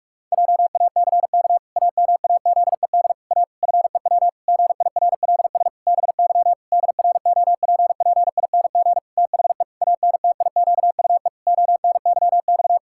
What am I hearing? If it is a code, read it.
Morse code, heard as a keyed tone, about 46 wpm, 700 Hz. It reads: JACK AMAZED A FEW GIRLS BY DROPPING THE ANTIQUE ONYX
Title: Pangram Challenge v2 - 45wpm